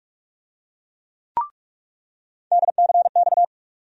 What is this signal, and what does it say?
Morse code, heard as a keyed tone, about 46 wpm, 700 Hz. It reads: DCX
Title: USA Call Sign Postfixes - 3 Letters 45wpm